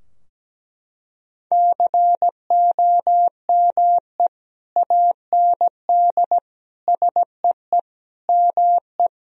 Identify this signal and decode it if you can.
Morse code, heard as a keyed tone, about 17 wpm, 700 Hz. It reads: COME AND SEE ME
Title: Sentences from Top 1000 Words - Review Part 2 - 17wpm